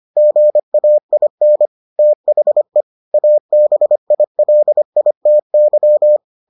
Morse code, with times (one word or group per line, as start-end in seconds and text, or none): 0.2-1.7 GAIN
2.0-2.8 THE
3.1-6.2 ABILITY